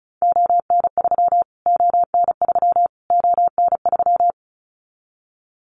Morse code, heard as a keyed tone, about 34 wpm, 700 Hz. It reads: ON3 ON3 ON3
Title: Callsign Prefixes for Top 25 DXCC - Repeated 3x - 35wpm